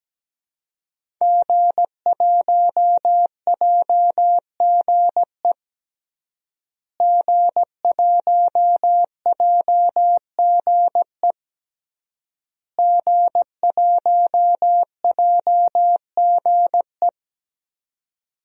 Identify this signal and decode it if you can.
Morse code, heard as a keyed tone, about 17 wpm, 700 Hz. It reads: G1JGE G1JGE G1JGE
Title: DXCC-100 Call Signs - 5 Characters - Repeated 3x - 17wpm